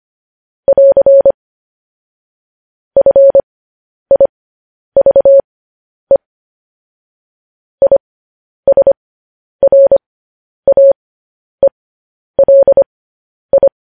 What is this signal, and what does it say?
Morse code, heard as a keyed tone, about 25 wpm, spaced as for 10 wpm, 550 Hz. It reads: <AR> FIVE ISRAELI
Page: news-headlines-in-morse-code-at-10-wpm